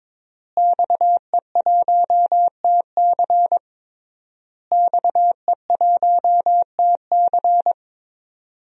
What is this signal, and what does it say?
Morse code, heard as a keyed tone, about 22 wpm, 700 Hz. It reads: XE1TC XE1TC
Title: DXCC-100 Call Signs - 5 Characters - Repeated 3x - 22wpm